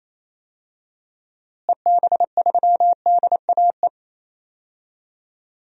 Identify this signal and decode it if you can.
Morse code, heard as a keyed tone, about 28 wpm, 700 Hz. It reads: EB3DAE
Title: DXCC-100 Call Signs - 6 Characters - Repeated 3x - 28wpm